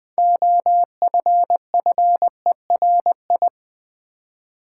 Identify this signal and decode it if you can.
Morse code, heard as a keyed tone, about 20 wpm, 700 Hz. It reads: OFFERI
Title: Build A Word - 3 Letters At A Time - 8-Letter Words - 20wpm